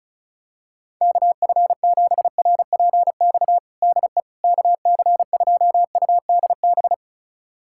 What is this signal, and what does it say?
Morse code, heard as a keyed tone, about 36 wpm, 700 Hz. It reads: KF7RPX DE KC2UDB
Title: USA Callsign DE Callsign - 6 Characters - 35wpm